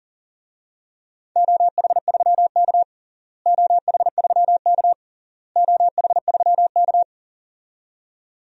Morse code, heard as a keyed tone, about 37 wpm, 700 Hz. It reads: OH3K OH3K OH3K
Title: DXCC-100 Call Signs - 4 Characters - Repeated 3x - 40wpm